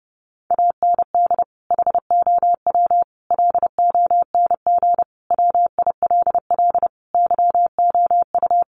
Morse code, heard as a keyed tone, about 30 wpm, 700 Hz. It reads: AND HOW LONG WILL YOU
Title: Sentences from Top 1200 Words 30wpm